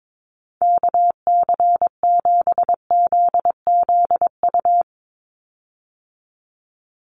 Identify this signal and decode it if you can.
Morse code, heard as a keyed tone, about 22 wpm, 700 Hz. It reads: KC7ZZU
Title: USA Call Signs 22wpm